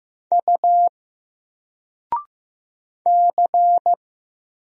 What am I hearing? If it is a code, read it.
Morse code, heard as a keyed tone, about 15 wpm, 700 Hz. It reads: U C
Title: Single Character (Letter:Number:Punctuation:Prosign) 15wpm